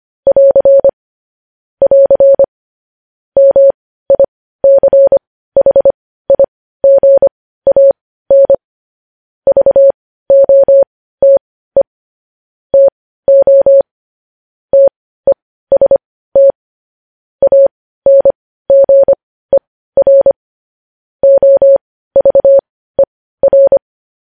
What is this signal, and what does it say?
Morse code, heard as a keyed tone, about 25 wpm, spaced as for 15 wpm, 550 Hz. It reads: <AR> <AR> MICHIGAN VOTE TO TEST ANGER OVER
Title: News Headlines at 15